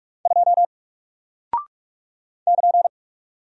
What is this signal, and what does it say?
Morse code, heard as a keyed tone, about 43 wpm, 700 Hz. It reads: 2 <KN>
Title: Single Character (Letter:Number:Punctuation:Prosign) Alt 45wpm